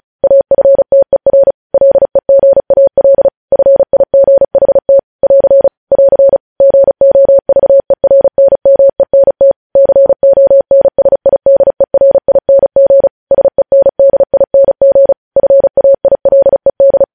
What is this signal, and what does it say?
Morse code, heard as a keyed tone, about 36 wpm, 550 Hz. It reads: AFTER LEGAL FIGHT <AR> <AR> GOVERNMENT CONSIDERING SENDING FAILED
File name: podcast / News headlines-1x-35x35